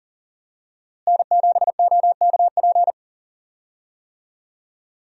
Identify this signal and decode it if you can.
Morse code, heard as a keyed tone, about 37 wpm, 700 Hz. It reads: N7OKP
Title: USA Call Signs - 5 Characters - Repeated 2x - 40wpm